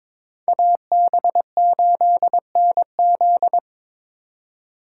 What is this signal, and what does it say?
Morse code, heard as a keyed tone, about 22 wpm, 700 Hz. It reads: AB8NZ
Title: USA Call Signs - 5 Characters - 2x Character Spacing - 22wpm